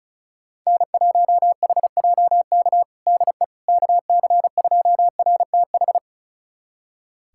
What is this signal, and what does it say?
Morse code, heard as a keyed tone, about 34 wpm, 700 Hz. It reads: N1HJK DE KC2RTH
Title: USA Callsign DE Callsign 35wpm